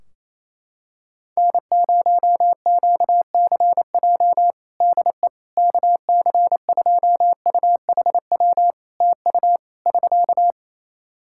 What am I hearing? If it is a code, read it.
Morse code, heard as a keyed tone, about 28 wpm, 700 Hz. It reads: N0QCJ DE KC2UHW TU <SK>
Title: K1USN SST QSO Practice 28wpm